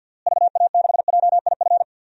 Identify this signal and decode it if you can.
Morse code, heard as a keyed tone, about 51 wpm, 700 Hz. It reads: VA6JIF
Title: DXCC-100 Call Signs - 6 Characters - Repeated 3x - 50wpm